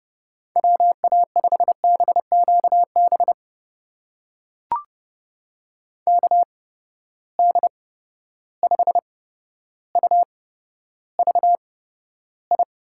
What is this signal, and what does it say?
Morse code, heard as a keyed tone, about 30 wpm, spaced as for 13 wpm, 700 Hz. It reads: WA5BQB KD5UVI
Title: USA Call Signs - 6 Characters - 8x Character Spacing - 30wpm